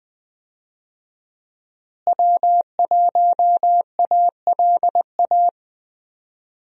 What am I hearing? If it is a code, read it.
Morse code, heard as a keyed tone, about 21 wpm, 700 Hz. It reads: W1ALA
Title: USA Call Signs - 5 Characters - Repeated 3x - 20wpm